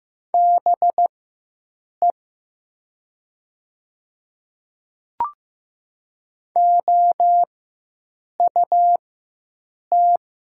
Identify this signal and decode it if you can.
Morse code, heard as a keyed tone, about 15 wpm, spaced as for 7 wpm, 700 Hz. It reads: BE OUT
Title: 4x Character Spacing - Top 100 Words - 15wpm